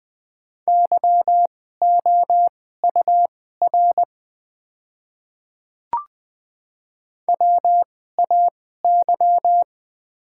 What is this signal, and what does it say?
Morse code, heard as a keyed tone, about 20 wpm, spaced as for 14 wpm, 700 Hz. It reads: YOUR WAY
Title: 2x Character Spacing - Top 100 Words - 20wpm